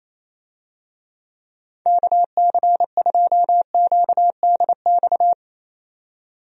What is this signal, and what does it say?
Morse code, heard as a keyed tone, about 28 wpm, 700 Hz. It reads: KC2QDX
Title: USA Call Signs - 6 Characters - 28wpm